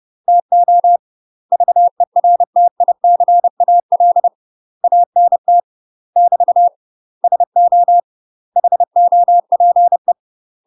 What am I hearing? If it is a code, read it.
Morse code, heard as a keyed tone, about 30 wpm, 700 Hz. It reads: TO VERTICAL ANT <BT> SO HOPE
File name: UK MASTER QSOs TRAINING / 30WPM / UK MASTER QSOs 0000